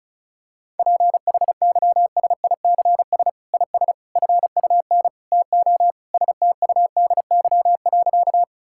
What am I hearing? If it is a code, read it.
Morse code, heard as a keyed tone, about 36 wpm, 700 Hz. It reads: PHYSICS IS FUN TO STUDY.
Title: Sentences from Top 1000 Words - Review Part 5 - 35wpm